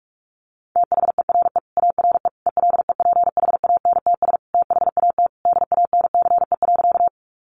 Morse code, heard as a keyed tone, about 44 wpm, 700 Hz. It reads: THERE ARE ELEPHANTS THAT DANCE.
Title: Easy Phrases 45wpm